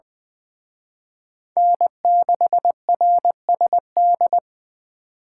Morse code, heard as a keyed tone, about 23 wpm, 700 Hz. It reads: N6RSD
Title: Speed Racing - USA 1x3 Call Signs 20wpm